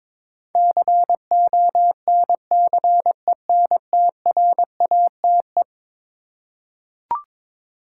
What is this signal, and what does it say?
Morse code, heard as a keyed tone, about 21 wpm, 700 Hz. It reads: CONCENTRATE
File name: practice / 22wpm / Eleven Letter Words - 22wpm